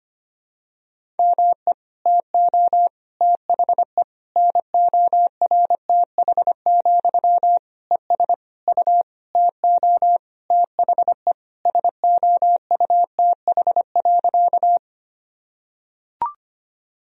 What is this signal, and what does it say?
Morse code, heard as a keyed tone, about 25 wpm, 700 Hz. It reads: ME TO THE NORTH, ES U TO THE SOUTH.
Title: Sentences from Top 400 Words and Abbreviations 25wpm